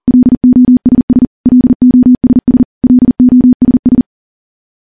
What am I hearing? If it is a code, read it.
Morse code, heard as a keyed tone, about 43 wpm, 250 Hz. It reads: LOSS LOSS LOSS
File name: morse-training-40WPM